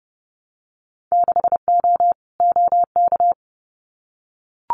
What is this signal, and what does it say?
Morse code, heard as a keyed tone, about 30 wpm, 700 Hz. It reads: 6O OK
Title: Winter Field Day Exchanges Encore 30wpm